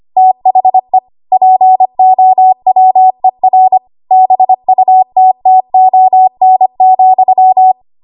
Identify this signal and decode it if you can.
Morse code, heard as a keyed tone, about 25 wpm, 750 Hz. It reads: THE POWER BUTTON,
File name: Archive / 25 WPM / 25WPM